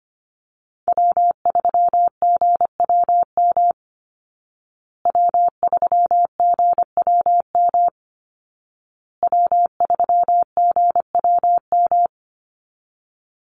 Morse code, heard as a keyed tone, about 25 wpm, 700 Hz. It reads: W3GWM W3GWM W3GWM
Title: Call Signs Repeated 2x - 25wpm